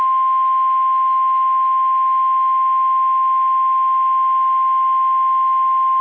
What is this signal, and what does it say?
Emergency Alert System (SAME) tones (the 1050 Hz attention tone). It is heard from 1.1 to 5.4 s.